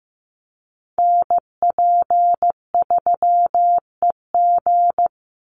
Morse code, heard as a keyed tone, about 15 wpm, 700 Hz. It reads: NP3EG
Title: USA Call Signs 15wpm